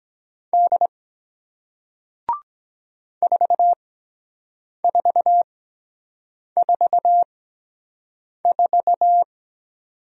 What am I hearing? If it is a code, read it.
Morse code, heard as a keyed tone, about 23 wpm, spaced as for 7 wpm, 700 Hz. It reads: D 4444